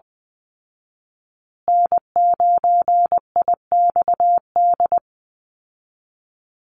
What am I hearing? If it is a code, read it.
Morse code, heard as a keyed tone, about 20 wpm, 700 Hz. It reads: N9IXD
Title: Call Signs Repeated 2x - 20wpm